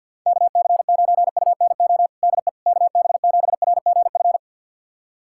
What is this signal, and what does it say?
Morse code, heard as a keyed tone, about 47 wpm, 700 Hz. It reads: KC9UNO DE KB7RGF
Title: USA Callsign DE Callsign 50wpm